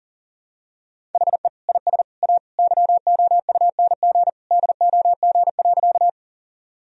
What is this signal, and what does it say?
Morse code, heard as a keyed tone, about 40 wpm, 700 Hz. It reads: HE IS A YOUNG DOG.
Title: Sentences from Top 1000 Words - Review Part 2 - 40wpm